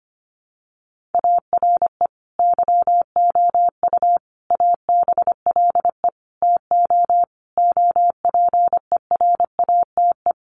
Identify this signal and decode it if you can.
Morse code, heard as a keyed tone, about 25 wpm, 700 Hz. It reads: ARE YOU ABLE TO OPERATE
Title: Sentences from Top 1500 Words 25wpm